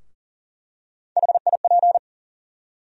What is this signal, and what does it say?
Morse code, heard as a keyed tone, about 40 wpm, 700 Hz. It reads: HIP
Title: USA Call Sign Postfixes - 3 Letters 40wpm